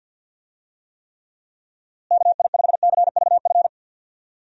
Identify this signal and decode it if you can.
Morse code, heard as a keyed tone, about 47 wpm, 700 Hz. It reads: KI5CVF
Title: USA Call Signs - 6 Characters - Repeated 2x - 50wpm